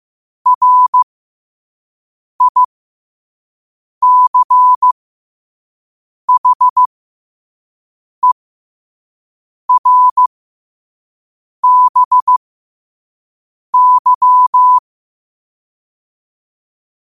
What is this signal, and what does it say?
Morse code, heard as a keyed tone, about 15 wpm, spaced as for 5 wpm, 1000 Hz. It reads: RICHER BY
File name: qotdfiles / 5wpm